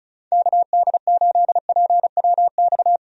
Morse code, heard as a keyed tone, about 36 wpm, 700 Hz. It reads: KD8PWX
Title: USA Call Signs - 6 Characters - 35wpm